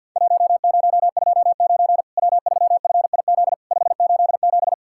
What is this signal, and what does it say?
Morse code, heard as a keyed tone, about 50 wpm, 700 Hz. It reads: 1029 W3FIZ 587
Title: Worked All Europe DX Contest - QTC Exchange - 50wpm